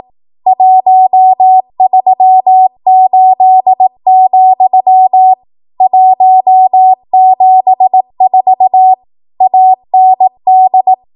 Morse code, heard as a keyed tone, about 18 wpm, 750 Hz. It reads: 138, 174 AND
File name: Archive / 18 WPM / 18WPM